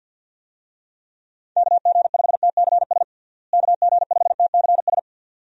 Morse code, heard as a keyed tone, about 52 wpm, 700 Hz. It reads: KG5TCS KG5TCS
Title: USA Call Signs - 6 Characters - Repeated 2x - 50wpm